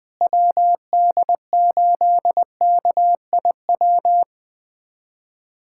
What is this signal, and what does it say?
Morse code, heard as a keyed tone, about 21 wpm, 700 Hz. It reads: WD8KIW
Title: USA Callsign DE Callsign - 6 Characters - 20wpm